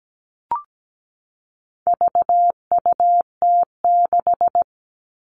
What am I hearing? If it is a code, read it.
Morse code, heard as a keyed tone, about 17 wpm, 700 Hz. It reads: VUT6
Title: All Cut Numbers - 4-Digit 17wpm